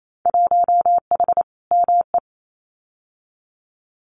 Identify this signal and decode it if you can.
Morse code, heard as a keyed tone, about 28 wpm, 700 Hz. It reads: 1H ME
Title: Winter Field Day Exchanges 28wpm